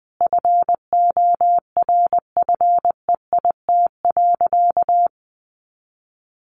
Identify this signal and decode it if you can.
Morse code, heard as a keyed tone, about 20 wpm, 700 Hz. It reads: FORFEIT.
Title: A Sound Of Thunder 20wpm